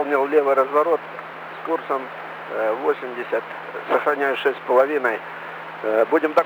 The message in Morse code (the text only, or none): none